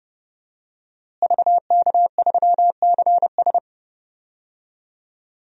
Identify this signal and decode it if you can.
Morse code, heard as a keyed tone, about 30 wpm, 700 Hz. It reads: VK3CS
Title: DXCC-100 Call Signs - 5 Characters - Repeated 2x - 30wpm